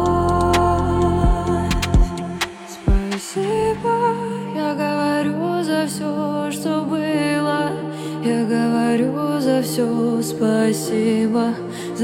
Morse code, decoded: none